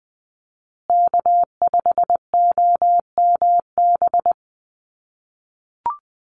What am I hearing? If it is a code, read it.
Morse code, heard as a keyed tone, about 20 wpm, 700 Hz. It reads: K5OMB
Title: USA Call Signs Repeated 2x 20wpm